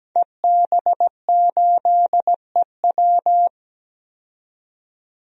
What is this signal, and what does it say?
Morse code, heard as a keyed tone, about 17 wpm, 700 Hz. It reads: EB8EW
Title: DXCC-100 Call Signs - 5 Characters - 17wpm